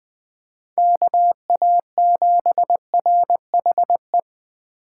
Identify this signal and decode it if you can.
Morse code, heard as a keyed tone, about 21 wpm, 700 Hz. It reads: KA7RHE